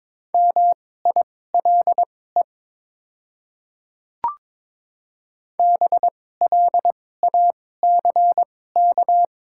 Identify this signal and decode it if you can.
Morse code, heard as a keyed tone, about 22 wpm, spaced as for 16 wpm, 700 Hz. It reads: MILE BLACK
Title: 2x Character Spacing - Top 500 Words - 22wpm